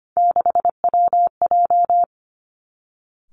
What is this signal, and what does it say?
Morse code, heard as a keyed tone, about 25 wpm, 700 Hz. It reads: 6WJ
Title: Three Challenging Letter-Number Combinations 25wpm